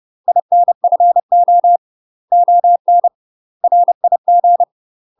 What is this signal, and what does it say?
Morse code, heard as a keyed tone, about 30 wpm, 700 Hz. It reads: INFO ON RIG
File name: UK MASTER QSOs TRAINING / 30WPM / UK MASTER QSOs 0015